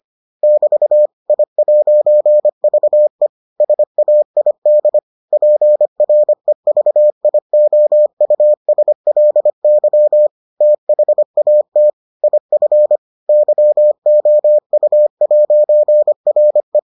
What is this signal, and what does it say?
Morse code, heard as a keyed tone, about 25 wpm, 600 Hz. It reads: <BT> I'VE SAID PREVIOUSLY THAT IF YOU'RE